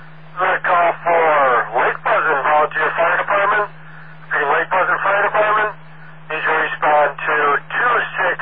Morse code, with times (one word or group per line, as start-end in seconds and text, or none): none